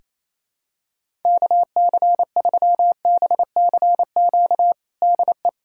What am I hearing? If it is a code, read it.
Morse code, heard as a keyed tone, about 28 wpm, 700 Hz. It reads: KC3BCQ DE
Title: USA Callsign DE Callsign 28wpm